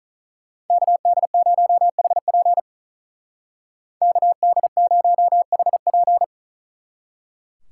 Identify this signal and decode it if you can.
Morse code, heard as a keyed tone, about 44 wpm, 700 Hz. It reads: KD0HP KD0HP
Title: Speed Racing - USA Call Signs 35wpm